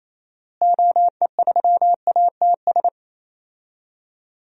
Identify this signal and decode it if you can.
Morse code, heard as a keyed tone, about 28 wpm, 700 Hz. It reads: OE3ATS
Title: DXCC-100 Call Signs - 6 Characters - Repeated 3x - 28wpm